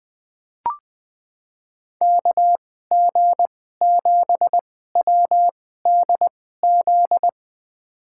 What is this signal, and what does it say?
Morse code, heard as a keyed tone, about 20 wpm, spaced as for 14 wpm, 700 Hz. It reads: KG7WDZ
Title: USA Call Signs - 6 Characters - 2x Character Spacing - 20wpm